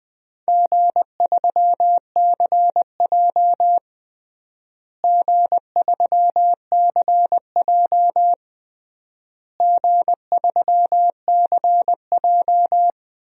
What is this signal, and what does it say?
Morse code, heard as a keyed tone, about 21 wpm, 700 Hz. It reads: G3CJ G3CJ G3CJ
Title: DXCC-100 Call Signs - 4 Characters - Repeated 3x - 20wpm